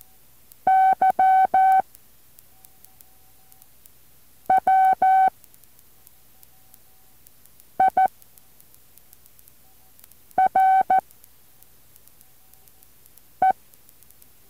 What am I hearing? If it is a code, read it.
Morse code, heard as a keyed tone, about 14 wpm, 760 Hz. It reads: Y W I R E
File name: intl morse code2